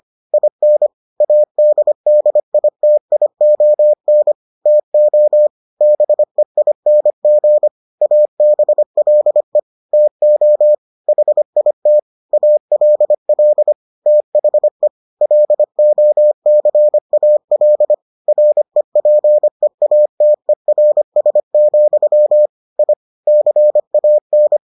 Morse code, heard as a keyed tone, about 25 wpm, 600 Hz. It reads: IN ADDITION TO BEING ABLE TO HIT ALL THE LOCAL REPEATERS, I CAN